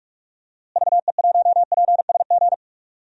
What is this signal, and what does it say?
Morse code, heard as a keyed tone, about 46 wpm, 700 Hz. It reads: VE1PSG
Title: DXCC-100 Call Signs - 6 Characters - Repeated 3x - 45wpm